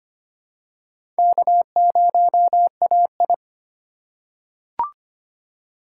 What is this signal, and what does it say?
Morse code, heard as a keyed tone, about 25 wpm, 700 Hz. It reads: K0AI
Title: USA Call Signs - 4 Characters - 2x Character Spacing - 25wpm